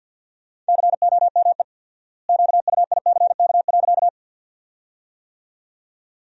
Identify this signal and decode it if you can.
Morse code, heard as a keyed tone, about 50 wpm, 700 Hz. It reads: COME QUICK.
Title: 4x Word-Spacing - Sentences from Top 500 Words - 50wpm